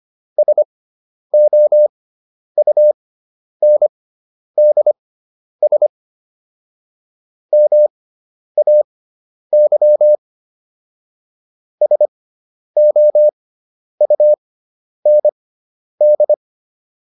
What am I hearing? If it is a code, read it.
Morse code, heard as a keyed tone, about 25 wpm, spaced as for 10 wpm, 600 Hz. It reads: SOUNDS MAY SOUND